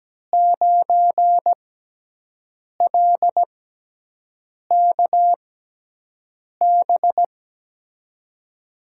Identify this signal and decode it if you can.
Morse code, heard as a keyed tone, about 17 wpm, spaced as for 6 wpm, 700 Hz. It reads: 9LKB
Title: USA Call Signs - 6 Characters - 6x Character Spacing - 17wpm